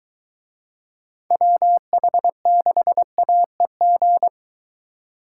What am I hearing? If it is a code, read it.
Morse code, heard as a keyed tone, about 22 wpm, 700 Hz. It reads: WH6AEG